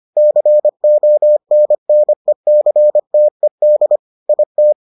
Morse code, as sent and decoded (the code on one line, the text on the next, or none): -.-. --- -. -. -.-. - . -.. / .. -
CONNECTED IT